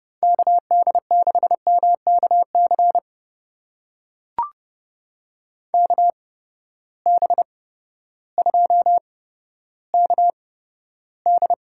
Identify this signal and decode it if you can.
Morse code, heard as a keyed tone, about 31 wpm, spaced as for 14 wpm, 700 Hz. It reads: KD6MKC KB2KD
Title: USA Call Signs - 6 Characters - 8x Character Spacing - 30wpm